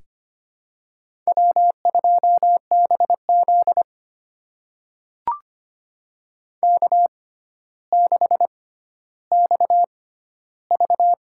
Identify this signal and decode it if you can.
Morse code, heard as a keyed tone, about 25 wpm, spaced as for 13 wpm, 700 Hz. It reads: W2BZ K6XV
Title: USA Call Signs - 4 Characters - 6x Character Spacing - 25wpm